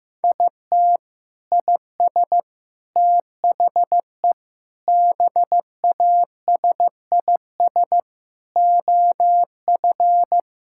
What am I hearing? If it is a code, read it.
Morse code, heard as a keyed tone, about 16 wpm, 700 Hz. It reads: IT IS THE BASIS OF
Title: Sentences from Top 1000 Words - Review Part 5 - 15wpm